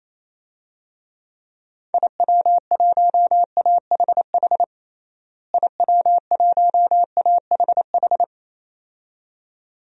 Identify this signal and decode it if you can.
Morse code, heard as a keyed tone, about 27 wpm, 700 Hz. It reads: IW1AHH IW1AHH
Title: DXCC-100 Call Signs - 6 Characters - Repeated 2x - 28wpm